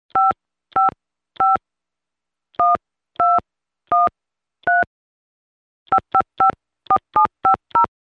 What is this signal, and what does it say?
DTMF, heard as telephone touch tones, tones of about 130 ms. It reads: 555 1213 5554750
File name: telephone / libsounds.com - Tone dialing sounds on the handset 3